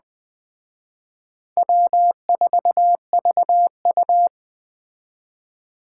Morse code, heard as a keyed tone, about 21 wpm, 700 Hz. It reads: W4VU
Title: Speed Racing - USA 1x2 Call Signs 17wpm